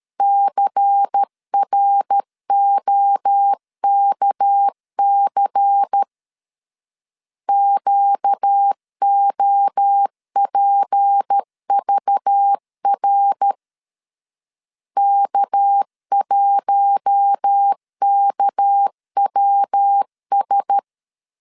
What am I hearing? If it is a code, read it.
Morse code, heard as a keyed tone, about 13 wpm, 800 Hz. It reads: CROKC QOPVR K1KWS
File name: LEZIONI / 6 PKX0 / 6 PK 12wpm 2006